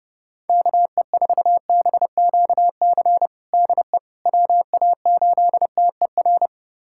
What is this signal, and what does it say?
Morse code, heard as a keyed tone, about 32 wpm, 700 Hz. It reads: KE4BQC DE WA8TER
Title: USA Callsign DE Callsign 30wpm